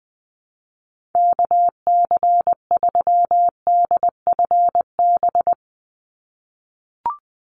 Morse code, heard as a keyed tone, about 20 wpm, 700 Hz. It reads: KC3DFB
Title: USA Call Signs 20wpm